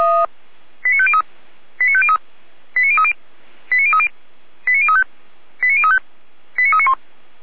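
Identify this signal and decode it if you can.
Five-tone selective call (ZVEI), heard as quick runs of five tones, tones of about 70 ms. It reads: 89503 89503 89030 89030 89035 89035 89391
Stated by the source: Five-tone ZVEI selective call 89050, received by SDR.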